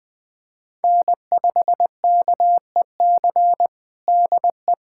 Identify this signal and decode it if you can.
Morse code, heard as a keyed tone, about 20 wpm, 700 Hz. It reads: N5KEC DE
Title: USA Callsign DE Callsign 20wpm